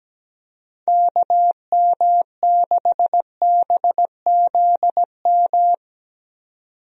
Morse code, heard as a keyed tone, about 17 wpm, 700 Hz. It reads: KM6BZM
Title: USA Call Signs - 6 Characters - 17wpm